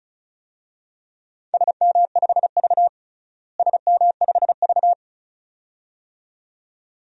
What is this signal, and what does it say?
Morse code, heard as a keyed tone, about 34 wpm, 700 Hz. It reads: SM5V SM5V
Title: DXCC-100 Call Signs - 4 Characters - Repeated 2x - 35wpm